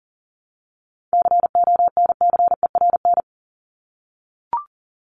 Morse code, heard as a keyed tone, about 40 wpm, 700 Hz. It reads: CONCERN
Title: ED Suffix - Top 100 Words with Base Words 40wpm